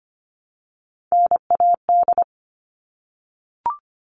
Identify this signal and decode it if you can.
Morse code, heard as a keyed tone, about 25 wpm, 700 Hz. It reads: NAD
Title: All Cut Numbers - 3-Digit 25wpm